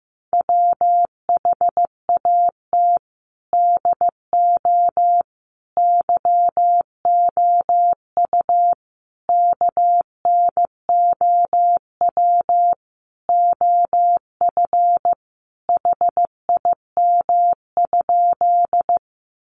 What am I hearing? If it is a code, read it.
Morse code, heard as a keyed tone, about 15 wpm, 700 Hz. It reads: WHAT DO YOU KNOW OF HIM?
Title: Sentences from Top 100 Words 15wpm